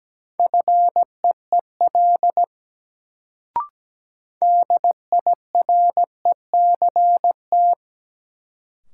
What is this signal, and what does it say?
Morse code, heard as a keyed tone, about 17 wpm, 700 Hz. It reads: FEEL DIRECT
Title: Sentences from Top 400 Words - New Word Drill - 17wpm